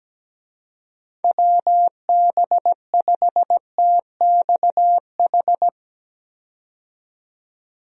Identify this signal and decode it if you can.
Morse code, heard as a keyed tone, about 17 wpm, 700 Hz. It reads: WB5TXH
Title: USA Call Signs - 6 Characters - 17wpm